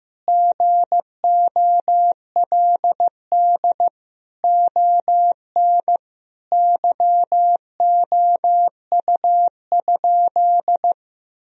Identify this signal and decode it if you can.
Morse code, heard as a keyed tone, about 16 wpm, 700 Hz. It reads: GOLD ON YOU?
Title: Sentences from Top 1000 Words - Review Part 3 - 15wpm